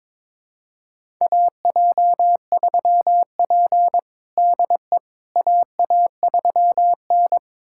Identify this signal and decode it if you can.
Morse code, heard as a keyed tone, about 22 wpm, 700 Hz. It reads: AJ3P DE AA3N
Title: USA Callsign DE Callsign - 4 Characters - 22wpm